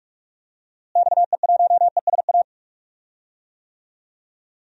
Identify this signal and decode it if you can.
Morse code, heard as a keyed tone, about 47 wpm, 700 Hz. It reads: XE1ESA
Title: DXCC-100 Call Signs - 6 Characters - Repeated 2x - 45wpm